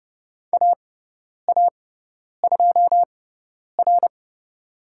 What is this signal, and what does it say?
Morse code, heard as a keyed tone, about 30 wpm, spaced as for 10 wpm, 700 Hz. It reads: AA2R
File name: Lesson 17g - Letter u - Call Signs 30wpm